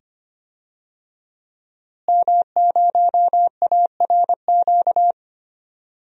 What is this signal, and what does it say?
Morse code, heard as a keyed tone, about 25 wpm, 700 Hz. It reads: M0ARQ
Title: DXCC-100 Call Signs - 5 Characters - Repeated 2x - 25wpm